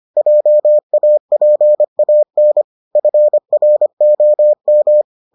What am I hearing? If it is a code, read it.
Morse code, heard as a keyed tone, about 25 wpm, 600 Hz. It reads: JAPAN FROM